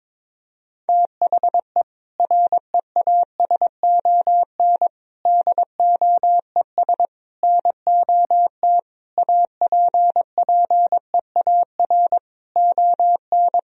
Morse code, heard as a keyed tone, about 22 wpm, 700 Hz. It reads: THE REASON DOES NOT APPEAR ON
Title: Sentences from Top 1000 Words - Review Part 3 - 22wpm